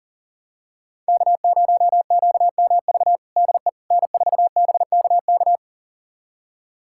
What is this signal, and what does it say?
Morse code, heard as a keyed tone, about 40 wpm, 700 Hz. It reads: K0QMV DE N4BKK
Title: USA Callsign DE Callsign - 5 Characters - 40wpm